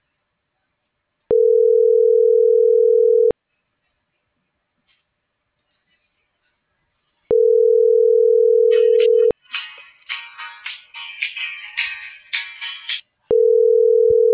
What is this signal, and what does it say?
Ringback tone (North America), 440 + 480 Hz.